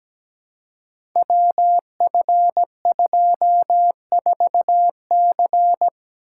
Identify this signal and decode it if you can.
Morse code, heard as a keyed tone, about 17 wpm, 700 Hz. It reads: WF24C
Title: Code Groups - 5 Letter:Numbers - 17wpm